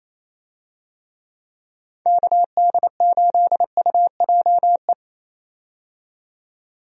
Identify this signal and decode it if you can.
Morse code, heard as a keyed tone, about 28 wpm, 700 Hz. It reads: KD8UJE
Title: USA Call Signs - 6 Characters - 28wpm